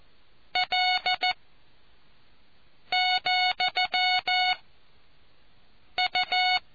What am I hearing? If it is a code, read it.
Morse code, heard as a keyed tone, about 14 wpm, 730 Hz. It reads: L , U